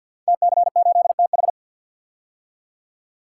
Morse code, heard as a keyed tone, about 53 wpm, 700 Hz. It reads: TC8TH
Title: Code Groups - 5 Letter:Numbers - 50wpm